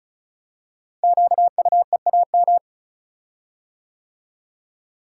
Morse code, heard as a keyed tone, about 36 wpm, 700 Hz. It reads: QUEAM
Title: Code Groups - 5 Letters - 35wpm